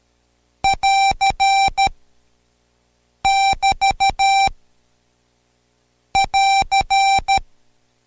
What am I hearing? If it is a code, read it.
Morse code, heard as a keyed tone, about 13 wpm, 780 Hz. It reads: <AR> <BT> <AR>
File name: Code Quick / AudioDownld2 / Seg127-Code Quick Prosigns 2